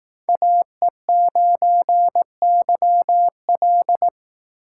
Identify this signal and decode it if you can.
Morse code, heard as a keyed tone, about 19 wpm, 700 Hz. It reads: AE9YL